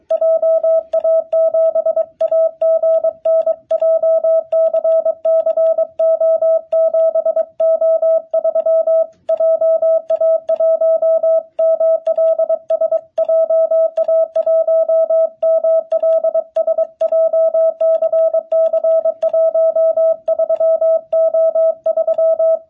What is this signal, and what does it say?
Morse code, heard as a keyed tone, about 24 wpm, 660 Hz. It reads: JA7AGN JCC O7O3 JA1MLS JA1MLS JCC 13O3
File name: jh8cbh / 2021cw